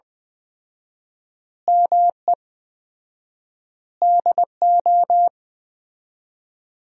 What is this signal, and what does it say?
Morse code, heard as a keyed tone, about 20 wpm, 700 Hz. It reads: ME DO
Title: 4x Word-Spacing - Sentences from Top 300 Words - 20wpm